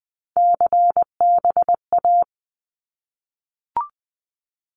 Morse code, heard as a keyed tone, about 20 wpm, 700 Hz. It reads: CBA
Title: Abbreviations 20wpm